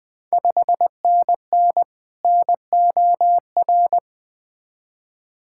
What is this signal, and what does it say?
Morse code, heard as a keyed tone, about 20 wpm, 700 Hz. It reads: 5NN NOR
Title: Ontario QSO Party Exchange 20wpm